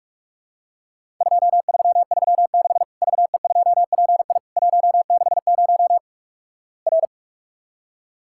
Morse code, heard as a keyed tone, about 48 wpm, 700 Hz. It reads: 2336 VE2PI 160 R
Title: Worked All Europe DX Contest - QTC Exchange - 45wpm